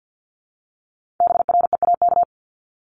Morse code, heard as a keyed tone, about 50 wpm, 700 Hz. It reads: BREAK
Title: Multi-letter Phonemes Repeated 3x 50wpm